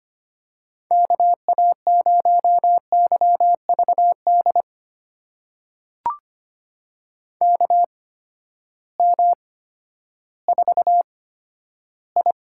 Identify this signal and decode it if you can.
Morse code, heard as a keyed tone, about 25 wpm, spaced as for 12 wpm, 700 Hz. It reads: KA0YVD KM4I
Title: USA Call Signs - 6 Characters - 8x Character Spacing - 25wpm